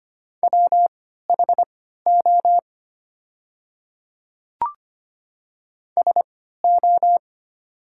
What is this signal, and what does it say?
Morse code, heard as a keyed tone, about 25 wpm, spaced as for 14 wpm, 690 Hz. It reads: WHO SO
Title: 3x Character Spacing - Top 100 Words - 25wpm